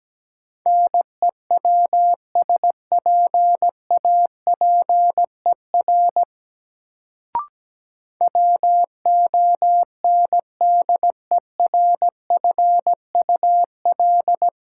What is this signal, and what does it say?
Morse code, heard as a keyed tone, about 17 wpm, 700 Hz. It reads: NEWSPAPER WONDERFUL
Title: Sentences from Top 1100 Words - New Word Drill Alt - 17wpm